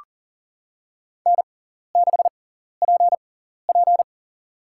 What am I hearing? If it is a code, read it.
Morse code, heard as a keyed tone, about 40 wpm, spaced as for 14 wpm, 700 Hz. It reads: N6PP
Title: USA Call Signs - 4 Characters - 6x Character Spacing - 40wpm